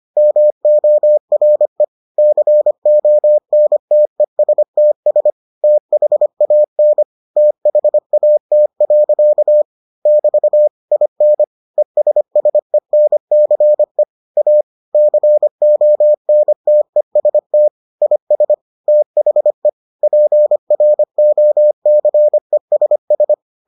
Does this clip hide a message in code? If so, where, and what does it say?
Yes: Morse code from 0.2 to 23.3 s, heard as a keyed tone, about 25 wpm, 600 Hz. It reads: MORE CONTESTS THAN THAT. <BT> IN ESSENCE A CONTEST IS THE PROCESS